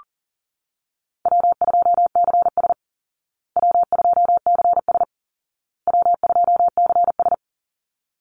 Morse code, heard as a keyed tone, about 40 wpm, 700 Hz. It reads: W2CS W2CS W2CS
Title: Easy Call Signs Repeated 3x - 40wpm